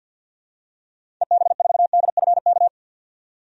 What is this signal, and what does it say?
Morse code, heard as a keyed tone, about 47 wpm, 700 Hz. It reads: EB4DFK
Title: DXCC-100 Call Signs - 6 Characters - 50wpm